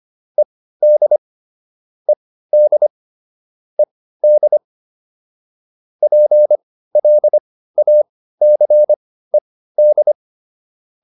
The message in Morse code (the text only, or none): ED ED ED PLACED